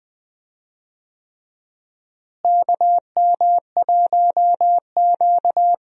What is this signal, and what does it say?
Morse code, heard as a keyed tone, about 20 wpm, 700 Hz. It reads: KM1Q
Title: USA Call Signs - 4 Characters - Repeated 3x - 20wpm